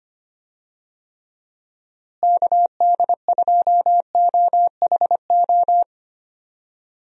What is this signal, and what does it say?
Morse code, heard as a keyed tone, about 25 wpm, 700 Hz. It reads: KD2OHO
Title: USA Call Signs - 6 Characters - Repeated 2x - 25wpm